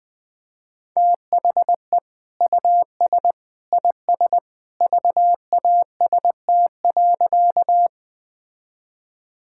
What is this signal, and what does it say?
Morse code, heard as a keyed tone, about 21 wpm, 700 Hz. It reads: THE US IS VAST.
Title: Sentences with 2-Word Phrases - 20wpm